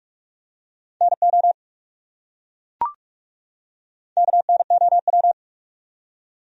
Morse code, heard as a keyed tone, about 42 wpm, 700 Hz. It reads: NO KNOW
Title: Top 100 Words 45wpm